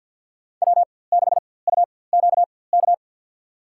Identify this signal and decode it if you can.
Morse code, heard as a keyed tone, about 53 wpm, spaced as for 24 wpm, 700 Hz. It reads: W6UQK
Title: USA Call Signs - 5 Characters - 4x Character Spacing - 50wpm